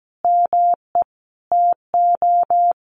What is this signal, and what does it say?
Morse code, heard as a keyed tone, about 17 wpm, 700 Hz. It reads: ME TO
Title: Sets of 4 Words 17wpm